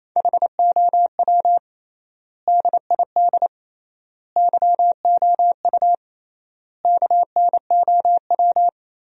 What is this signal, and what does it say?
Morse code, heard as a keyed tone, about 28 wpm, 700 Hz. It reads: HOW DID YOU KNOW
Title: 3x Word-Spacing - Sentences from Top 100 Words - 28wpm